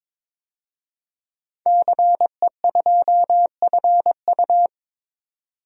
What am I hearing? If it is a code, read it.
Morse code, heard as a keyed tone, about 22 wpm, 700 Hz. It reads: CE2FU
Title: DXCC-100 Call Signs - 5 Characters - 22wpm